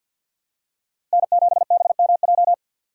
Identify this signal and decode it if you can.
Morse code, heard as a keyed tone, about 53 wpm, 700 Hz. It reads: N7BMJ
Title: USA Call Signs - 5 Characters - 2x Character Spacing - 50wpm